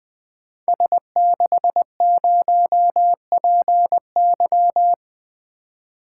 Morse code, heard as a keyed tone, about 20 wpm, 700 Hz. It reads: S60PY
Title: Code Groups - 5 Letter:Numbers - 20wpm